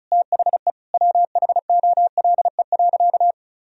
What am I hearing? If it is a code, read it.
Morse code, heard as a keyed tone, about 36 wpm, 700 Hz. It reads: THE WHOLE.
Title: Sentences from Top 1000 Words - Review Part 1 - 35wpm